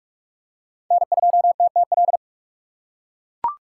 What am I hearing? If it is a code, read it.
Morse code, heard as a keyed tone, about 44 wpm, 700 Hz. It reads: N2TTL
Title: USA Call Signs - 5 Characters - 6x Character Spacing - 45wpm